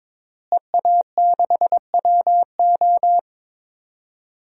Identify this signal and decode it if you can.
Morse code, heard as a keyed tone, about 23 wpm, 700 Hz. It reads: EA6WO
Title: DXCC-100 Call Signs - 5 Characters - Repeated 3x - 22wpm